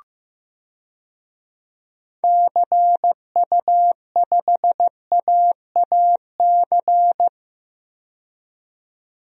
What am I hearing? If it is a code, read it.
Morse code, heard as a keyed tone, about 16 wpm, 700 Hz. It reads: CU5AAC
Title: DXCC-100 Call Signs - 6 Characters - 15wpm